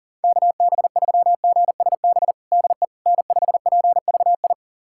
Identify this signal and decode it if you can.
Morse code, heard as a keyed tone, about 40 wpm, 700 Hz. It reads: KB3GSB DE N5PVI
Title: USA Callsign DE Callsign 40wpm